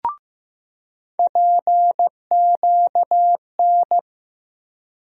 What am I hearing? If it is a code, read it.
Morse code, heard as a keyed tone, about 15 wpm, 700 Hz. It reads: PQN
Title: USA Call Sign Postfixes - 3 Letters 15wpm